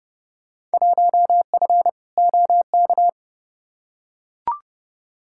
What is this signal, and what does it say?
Morse code, heard as a keyed tone, about 30 wpm, 700 Hz. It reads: 1F OK
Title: Field Day Exchanges Encore - 30wpm